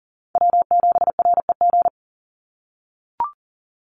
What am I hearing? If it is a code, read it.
Morse code, heard as a keyed tone, about 40 wpm, 700 Hz. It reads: W7REG
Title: USA Call Signs 40wpm